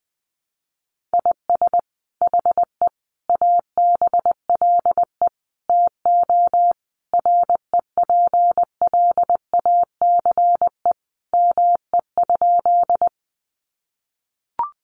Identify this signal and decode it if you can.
Morse code, heard as a keyed tone, about 20 wpm, 700 Hz. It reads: IS HE ABLE TO REPLACE ME?
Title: Sentences from Top 1400 Words 20wpm